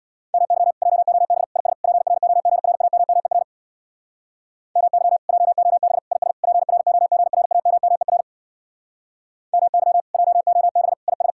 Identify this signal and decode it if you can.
Morse code, heard as a keyed tone, about 61 wpm, 700 Hz. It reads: MY JOB IS PROGRAMMER MY JOB IS PROGRAMMER MY JOB IS
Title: Speed-Racing Common QSO Phrases Encore - 50wpm